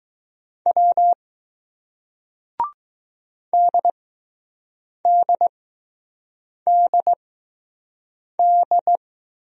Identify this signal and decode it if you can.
Morse code, heard as a keyed tone, about 20 wpm, spaced as for 6 wpm, 700 Hz. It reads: W DDDD